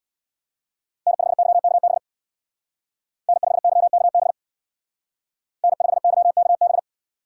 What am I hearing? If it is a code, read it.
Morse code, heard as a keyed tone, about 61 wpm, 700 Hz. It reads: N5YKB N5YKB N5YKB
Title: Speed Racing - USA Call Signs 50wpm